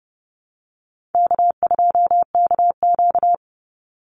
Morse code, heard as a keyed tone, about 31 wpm, 700 Hz. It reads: K2KQ
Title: Call Signs - 30wpm